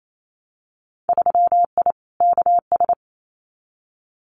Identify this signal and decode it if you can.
Morse code, heard as a keyed tone, about 28 wpm, 700 Hz. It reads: 3I KS
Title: Winter Field Day Exchanges 28wpm